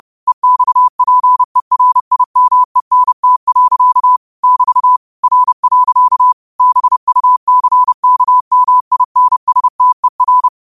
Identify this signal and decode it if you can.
Morse code, heard as a keyed tone, about 30 wpm, 1000 Hz. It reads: EXPERIMENT. <BT> R. BUCKMINSTER
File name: qotdfiles / 30wpm